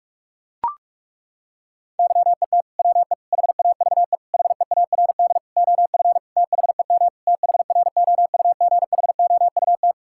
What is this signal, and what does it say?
Morse code, heard as a keyed tone, about 41 wpm, 690 Hz. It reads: YET WE HAVE HEARD OF THEM THROUGHOUT
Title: Sentences from Top 1000 Words - Review Part 5 - 45wpm